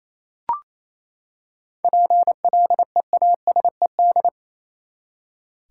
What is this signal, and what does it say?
Morse code, heard as a keyed tone, about 28 wpm, 700 Hz. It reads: PLEASED
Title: Seven Letter Words - 28wpm